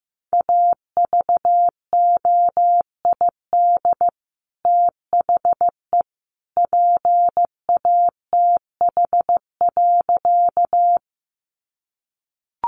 Morse code, heard as a keyed tone, about 15 wpm, 700 Hz. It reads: AVOID THE PATH.
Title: Sentences from Top 1400 Words 15wpm